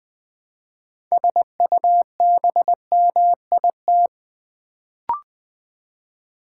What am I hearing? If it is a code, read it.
Morse code, heard as a keyed tone, about 20 wpm, 700 Hz. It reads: SUBMIT
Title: Build A Word - 2 Letters At A Time - 6-Letter Words - 20wpm